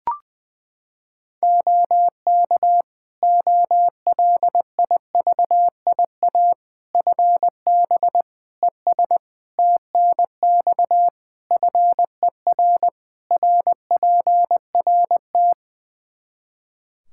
Morse code, heard as a keyed tone, about 20 wpm, 700 Hz. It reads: OK OLIVIA FB ES TNX FER RPRT
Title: Practice QSO - Exchange 2 - Info - 20wpm